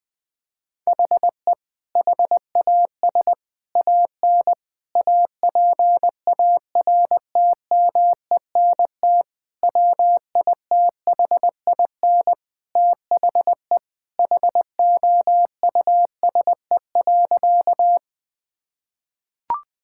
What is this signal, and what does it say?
Morse code, heard as a keyed tone, about 20 wpm, 700 Hz. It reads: HE HAS AN APARTMENT WITHIN THE HOUSE.
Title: Sentences from Top 1000 Words - Review Part 6 - 20wpm